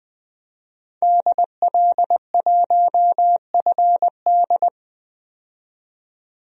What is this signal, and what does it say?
Morse code, heard as a keyed tone, about 20 wpm, 700 Hz. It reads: DL1FD
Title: DXCC-100 Call Signs - 5 Characters - 20wpm